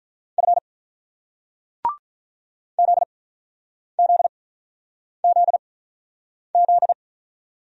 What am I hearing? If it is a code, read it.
Morse code, heard as a keyed tone, about 48 wpm, spaced as for 9 wpm, 700 Hz. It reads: F ZZZZ